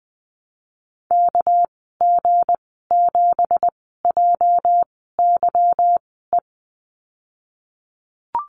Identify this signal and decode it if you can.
Morse code, heard as a keyed tone, about 19 wpm, spaced as for 14 wpm, 700 Hz. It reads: KG7JYE
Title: 231 Lesson 37g - Letter x - Call Signs 20wpm